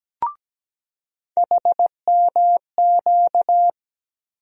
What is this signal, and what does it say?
Morse code, heard as a keyed tone, about 17 wpm, 700 Hz. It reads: HMQ
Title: USA Call Sign Postfixes - 3 Letters 17wpm